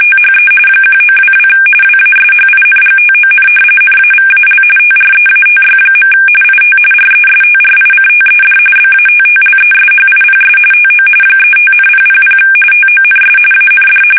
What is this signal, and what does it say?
RTTY, about 45.45 baud. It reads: THE 1922 WASHINGTON NAVAL TREATY. IT RESTRICTED BATTLESHIPS TO 35,000 TONS AND
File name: fsk850 / battleshipnj